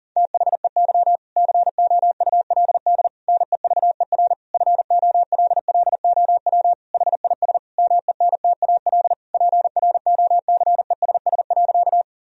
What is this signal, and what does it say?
Morse code, heard as a keyed tone, about 40 wpm, 690 Hz. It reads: THEY COULD NEVER FOLLOW HIS MENTAL PROCESS.
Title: Sentences from Top 1000 Words - Review Part 4 - 40wpm